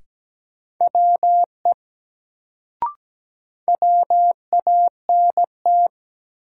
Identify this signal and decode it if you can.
Morse code, heard as a keyed tone, about 17 wpm, 700 Hz. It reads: WE WANT
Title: Sets of 2 Words - New Word Drill - 17wpm